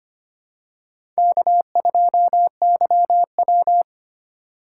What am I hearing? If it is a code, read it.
Morse code, heard as a keyed tone, about 25 wpm, 700 Hz. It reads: K2YW
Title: USA Call Signs - 4 Characters - 2x Character Spacing - 25wpm